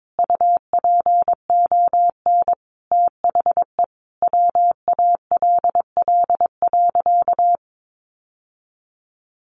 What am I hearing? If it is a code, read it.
Morse code, heard as a keyed tone, about 22 wpm, 700 Hz. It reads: UPON THE WALL.
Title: Sentences from Top 1100 Words 22wpm